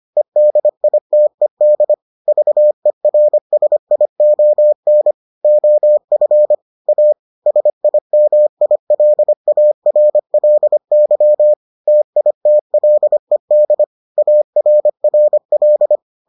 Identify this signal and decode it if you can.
Morse code, heard as a keyed tone, about 25 wpm, 600 Hz. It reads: EDITED VERSION OF A SIMILARLY TITLED ARRL